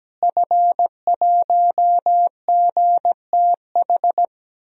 Morse code, heard as a keyed tone, about 17 wpm, 700 Hz. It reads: F1GTH
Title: DXCC-100 Call Signs - 5 Characters - Repeated 3x - 17wpm